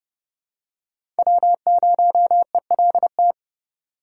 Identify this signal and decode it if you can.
Morse code, heard as a keyed tone, about 30 wpm, 700 Hz. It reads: W0ELT
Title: USA Call Signs - 3x Character Spacing - 30wpm